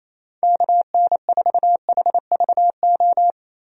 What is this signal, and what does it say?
Morse code, heard as a keyed tone, about 28 wpm, 700 Hz. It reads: KN4HVO
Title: USA Call Signs - 6 Characters - 28wpm